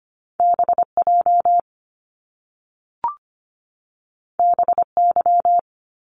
Morse code, heard as a keyed tone, about 25 wpm, 700 Hz. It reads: BJ BY
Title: Top Two Letter Combinations in Words 25wpm